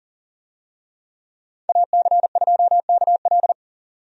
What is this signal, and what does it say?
Morse code, heard as a keyed tone, about 40 wpm, 700 Hz. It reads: AC2KL
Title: USA Call Signs - 5 Characters - 40wpm